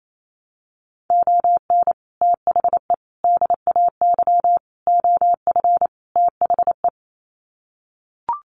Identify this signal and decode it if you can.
Morse code, heard as a keyed tone, about 28 wpm, 700 Hz. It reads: ON THE DAY OF THE
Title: Sets of 5 Words Encore 28wpm